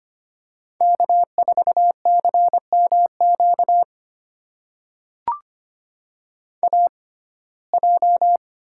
Morse code, heard as a keyed tone, about 25 wpm, spaced as for 18 wpm, 700 Hz. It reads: K4CMQ AJ
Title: USA Call Signs - 5 Characters - 6x Character Spacing - 25wpm